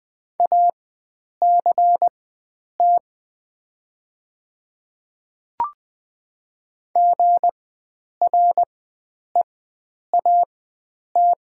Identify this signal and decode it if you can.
Morse code, heard as a keyed tone, about 21 wpm, spaced as for 9 wpm, 700 Hz. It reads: ACT GREAT
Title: 4x Character Spacing - Top 500 Words - 20wpm